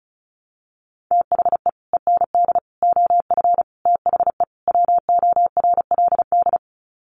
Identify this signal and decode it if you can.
Morse code, heard as a keyed tone, about 34 wpm, 700 Hz. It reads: THE END OF THE WORLD
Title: Sets of 5 Words 35wpm